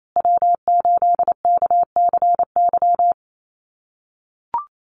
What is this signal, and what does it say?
Morse code, heard as a keyed tone, about 28 wpm, 700 Hz. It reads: W8KCY
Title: USA Call Signs Repeated 3x 28wpm